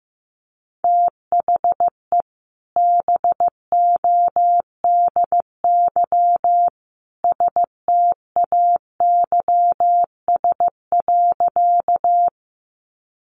Morse code, heard as a keyed tone, about 15 wpm, 700 Hz. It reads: THE BODY STAYS.
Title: A Sound Of Thunder 15wpm